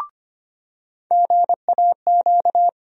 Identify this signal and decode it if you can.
Morse code, heard as a keyed tone, about 25 wpm, 700 Hz. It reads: GAQ
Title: USA Call Sign Postfixes - 3 Letters 25wpm